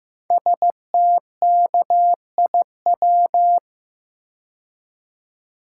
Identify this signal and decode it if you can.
Morse code, heard as a keyed tone, about 15 wpm, 700 Hz. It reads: STKIW
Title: Code Groups - 5 Letters - 15wpm